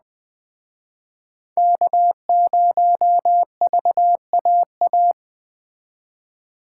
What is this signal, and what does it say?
Morse code, heard as a keyed tone, about 21 wpm, 700 Hz. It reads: K0VAA